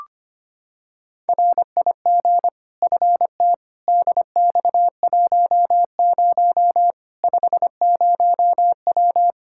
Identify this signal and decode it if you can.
Morse code, heard as a keyed tone, about 25 wpm, 700 Hz. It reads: RIG FT DX10 50W
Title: Practice QSO - Exchange 2 - Info - 25wpm